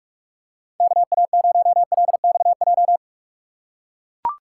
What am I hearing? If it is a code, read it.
Morse code, heard as a keyed tone, about 44 wpm, 700 Hz. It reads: KA0LXJ